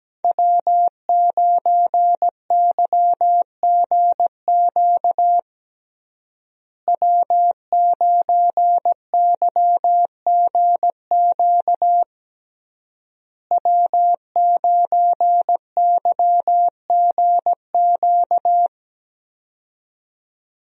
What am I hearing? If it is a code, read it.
Morse code, heard as a keyed tone, about 17 wpm, 700 Hz. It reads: W9YGQ W9YGQ W9YGQ
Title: USA Call Signs - 5 Characters - Repeated 3x - 17wpm